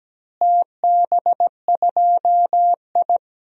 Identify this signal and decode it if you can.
Morse code, heard as a keyed tone, about 17 wpm, 700 Hz. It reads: TB2I
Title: DXCC-100 Call Signs - 4 Characters - Repeated 3x - 17wpm